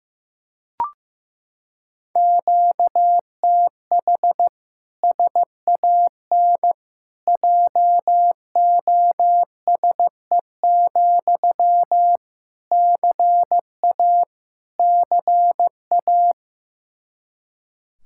Morse code, heard as a keyed tone, about 16 wpm, 700 Hz. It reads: QTH SAN JOSE, CA CA
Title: Practice QSO - Exchange 1 - Beginning - 15wpm